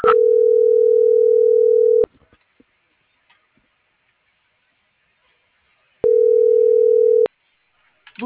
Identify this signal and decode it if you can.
Ringback tone (North America), 440 + 480 Hz.